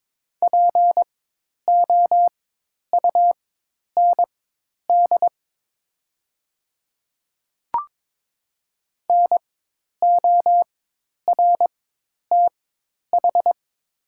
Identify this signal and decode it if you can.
Morse code, heard as a keyed tone, about 22 wpm, spaced as for 10 wpm, 700 Hz. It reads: POUND NORTH
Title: 4x Character Spacing - Top 500 Words - 22wpm